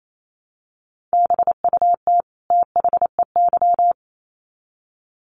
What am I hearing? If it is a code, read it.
Morse code, heard as a keyed tone, about 28 wpm, 700 Hz. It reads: BUT THEY
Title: Sets of 2 Words 28wpm